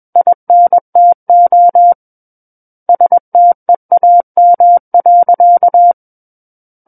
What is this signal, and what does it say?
Morse code, heard as a keyed tone, about 21 wpm, 700 Hz. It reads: INTO STEAM.
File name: Book One / 12-Chapter 12 - 21 WPM